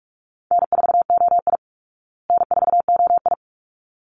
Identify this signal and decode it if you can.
Morse code, heard as a keyed tone, about 44 wpm, 700 Hz. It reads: N4OI N4OI
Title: Easy Call Signs Repeated 2x - 45wpm